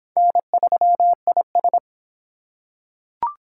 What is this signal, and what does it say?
Morse code, heard as a keyed tone, about 25 wpm, 700 Hz. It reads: N3IS